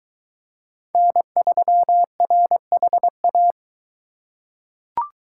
Morse code, heard as a keyed tone, about 21 wpm, 700 Hz. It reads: N3RHA